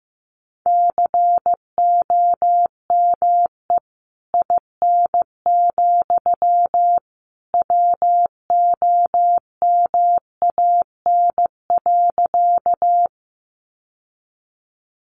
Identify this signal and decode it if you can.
Morse code, heard as a keyed tone, about 15 wpm, 700 Hz. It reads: COME IN, WOMAN.
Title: Sentences from Top 1100 Words 15wpm